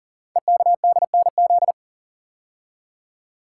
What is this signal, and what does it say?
Morse code, heard as a keyed tone, about 39 wpm, 700 Hz. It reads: EKDNZ
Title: Code Groups - 5 Letters - 40wpm